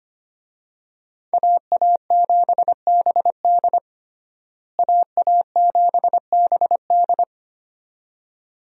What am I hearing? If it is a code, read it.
Morse code, heard as a keyed tone, about 25 wpm, 690 Hz. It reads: AA7BD AA7BD
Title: USA Call Signs - 5 Characters - Repeated 2x - 25wpm